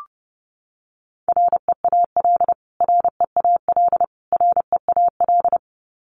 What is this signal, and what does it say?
Morse code, heard as a keyed tone, about 30 wpm, 700 Hz. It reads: REAL REAL REAL
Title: Top 500 Words Repeated 30wpm